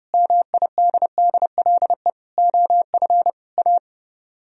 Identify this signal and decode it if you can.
Morse code, heard as a keyed tone, about 39 wpm, 700 Hz. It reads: MIDDLE OF A